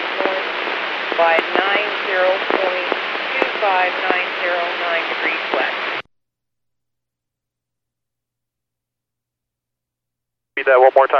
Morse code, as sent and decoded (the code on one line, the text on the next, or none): none